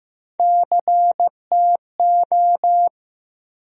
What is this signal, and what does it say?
Morse code, heard as a keyed tone, about 15 wpm, 700 Hz. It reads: CTO
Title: USA Call Sign Postfixes - 3 Letters 15wpm